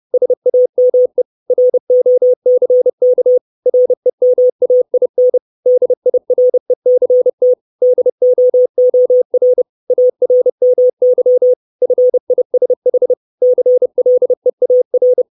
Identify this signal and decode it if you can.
Morse code, heard as a keyed tone, about 30 wpm, 500 Hz. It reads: SAME ROCK REMAIN DIRECT DOOR ARMY FISH CLEAR
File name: 632 Words 30WPM